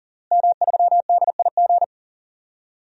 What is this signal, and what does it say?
Morse code, heard as a keyed tone, about 40 wpm, 700 Hz. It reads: M3DIG
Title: DXCC-100 Call Signs - 5 Characters - Repeated 3x - 40wpm